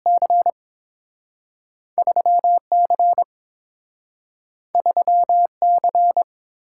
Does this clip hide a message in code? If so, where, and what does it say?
Yes: Morse code from 0.1 to 6.2 s, heard as a keyed tone, about 30 wpm, spaced as for 6 wpm, 700 Hz. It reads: C3 C3 C